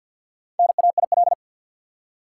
Morse code, heard as a keyed tone, about 50 wpm, 700 Hz. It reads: NAIL
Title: Four Letter Words - 50wpm